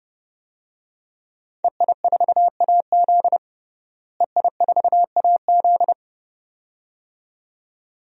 Morse code, heard as a keyed tone, about 30 wpm, 700 Hz. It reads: EI4AZ EI4AZ
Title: DXCC-100 Call Signs - 5 Characters - Repeated 2x - 30wpm